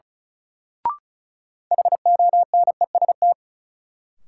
Morse code, heard as a keyed tone, about 36 wpm, 700 Hz. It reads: HONEST
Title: Sets of 5 Words Encore - New Word Drill - 35wpm